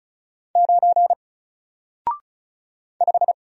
Morse code, heard as a keyed tone, about 36 wpm, 700 Hz. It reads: 9 5
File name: Single-Letter-Number-35wpm